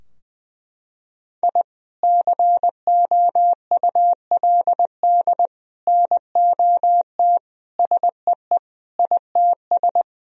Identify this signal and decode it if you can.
Morse code, heard as a keyed tone, about 20 wpm, 700 Hz. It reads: I COULD NOT SEE ITS
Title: Sentences from Top 1000 Words - Review Part 3 - 20wpm